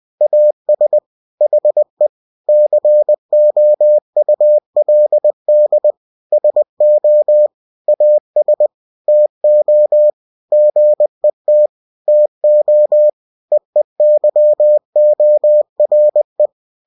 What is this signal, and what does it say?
Morse code, heard as a keyed tone, about 20 wpm, 600 Hz. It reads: AS HE COULD SO AS TO GET TO EEYORE